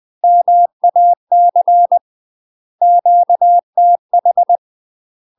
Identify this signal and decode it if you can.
Morse code, heard as a keyed tone, about 20 wpm, 700 Hz. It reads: MAC QTH
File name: UK MASTER QSOs TRAINING / 20WPM / UK MASTER QSOs 0072